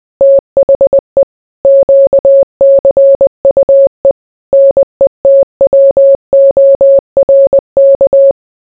Morse code, heard as a keyed tone, about 20 wpm, 550 Hz. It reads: THE QCUE NETWORK